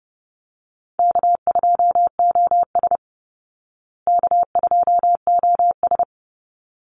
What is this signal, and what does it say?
Morse code, heard as a keyed tone, about 30 wpm, 700 Hz. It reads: K2OS K2OS
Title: Call Signs Repeated 3x - 30wpm